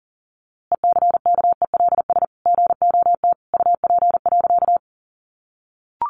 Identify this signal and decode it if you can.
Morse code, heard as a keyed tone, about 40 wpm, 700 Hz. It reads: ECKELS GOT UP.
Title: A Sound Of Thunder 40wpm